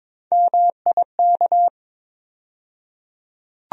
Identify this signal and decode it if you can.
Morse code, heard as a keyed tone, about 22 wpm, 700 Hz. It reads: MIK
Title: USA Call Sign Postfixes - 3 Letters 22wpm